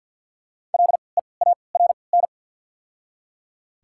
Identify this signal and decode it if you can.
Morse code, heard as a keyed tone, about 54 wpm, spaced as for 29 wpm, 690 Hz. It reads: LEARN
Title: 3x Character Spacing - Top 300 Words - 50wpm